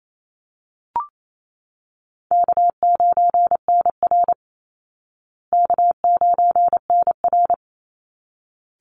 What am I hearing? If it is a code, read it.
Morse code, heard as a keyed tone, about 28 wpm, 700 Hz. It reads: K9NR K9NR
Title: Easy Call Signs Repeated 2x - 28wpm